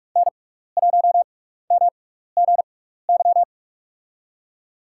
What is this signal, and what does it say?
Morse code, heard as a keyed tone, about 46 wpm, spaced as for 16 wpm, 700 Hz. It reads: N1MGY
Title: USA Call Signs - 6x Character Spacing - 45wpm